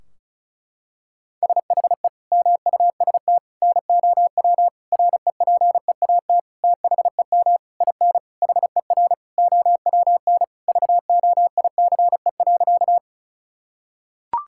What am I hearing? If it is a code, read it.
Morse code, heard as a keyed tone, about 36 wpm, 700 Hz. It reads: SHE MUST NOW REPEAT THEM IN HER OWN VOICE.
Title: Sentences from Top 1600 Words 35wpm